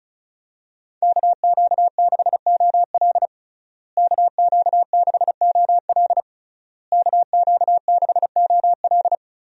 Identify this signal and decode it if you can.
Morse code, heard as a keyed tone, about 36 wpm, 700 Hz. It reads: KQ6OL KQ6OL KQ6OL
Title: USA Call Signs - 5 Characters - Repeated 3x - 35wpm